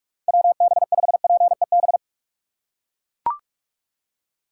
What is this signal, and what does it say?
Morse code, heard as a keyed tone, about 43 wpm, 700 Hz. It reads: WB5PEB
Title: USA Call Signs - 6 Characters - 3x Character Spacing - 45wpm